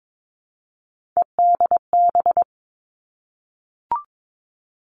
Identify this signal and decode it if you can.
Morse code, heard as a keyed tone, about 22 wpm, 700 Hz. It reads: EDB
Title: Three Challenging Letter-Number Combinations 22wpm